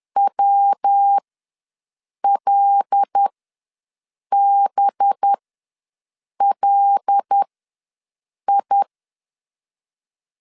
Morse code, heard as a keyed tone, about 10 wpm, spaced as for 6 wpm, 800 Hz. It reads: WLBLI
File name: LEZIONI / 5 LBVC / 5 LB ws 10wpm 3616